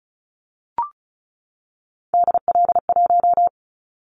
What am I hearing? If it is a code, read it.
Morse code, heard as a keyed tone, about 36 wpm, 700 Hz. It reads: DL1
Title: Callsign Prefixes for Top 50 DXCC - 35wpm